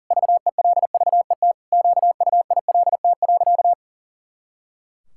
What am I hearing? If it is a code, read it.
Morse code, heard as a keyed tone, about 37 wpm, 700 Hz. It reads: VELVET QUILT.
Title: Pangram Challenge v2 - 40wpm